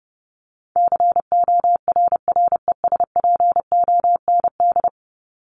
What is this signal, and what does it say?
Morse code, heard as a keyed tone, about 30 wpm, 700 Hz. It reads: CORRESPOND
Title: ING Suffix - Top 100 Words with Base Words 30wpm